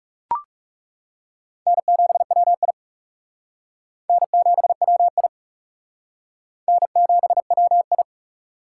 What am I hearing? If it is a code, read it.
Morse code, heard as a keyed tone, about 37 wpm, 700 Hz. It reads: N7WI N7WI N7WI
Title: Speed Racing - USA 1x2 Call Signs 30wpm